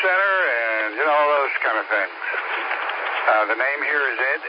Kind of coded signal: Emergency Alert System (SAME) tones